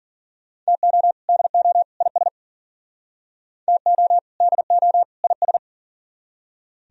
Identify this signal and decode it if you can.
Morse code, heard as a keyed tone, about 51 wpm, 700 Hz. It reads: TO DO IS TO DO IS